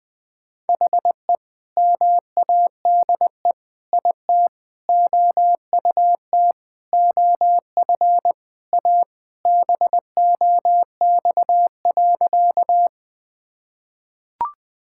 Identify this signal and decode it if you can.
Morse code, heard as a keyed tone, about 20 wpm, 700 Hz. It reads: HE MADE IT OUT OF A BOX.
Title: Sentences with 3-Word Phrases - 20wpm